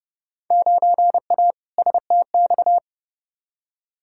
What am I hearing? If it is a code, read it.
Morse code, heard as a keyed tone, about 30 wpm, 700 Hz. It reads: 9A STX
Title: Field Day Exchanges - 30wpm